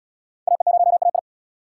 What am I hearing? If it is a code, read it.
Morse code, heard as a keyed tone, about 75 wpm, 700 Hz. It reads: AE0AI